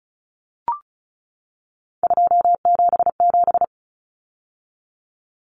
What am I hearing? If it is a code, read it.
Morse code, heard as a keyed tone, about 34 wpm, 700 Hz. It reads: 277
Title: Common Cut Numbers - 3-Digit 35wpm